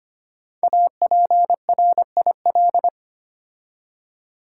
Morse code, heard as a keyed tone, about 25 wpm, 700 Hz. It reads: APRIL
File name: 5 Letter Words - First 3 Letters Given - 25wpm